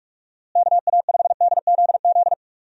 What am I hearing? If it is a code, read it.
Morse code, heard as a keyed tone, about 45 wpm, 700 Hz. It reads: KA5DZZ
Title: USA Call Signs - 6 Characters - Repeated 2x - 45wpm